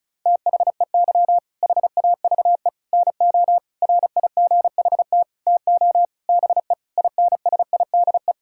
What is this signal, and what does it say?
Morse code, heard as a keyed tone, about 34 wpm, 700 Hz. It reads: THEY HAVE NO RIGHT TO BE INSIDE
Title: Sentences from Top 1000 Words - Review Part 6 - 35wpm